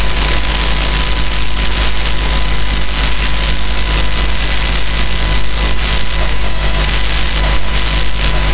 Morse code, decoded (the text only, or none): none